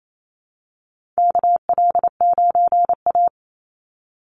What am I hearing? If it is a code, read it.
Morse code, heard as a keyed tone, about 28 wpm, 700 Hz. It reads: KL9A
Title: Call Signs - 28wpm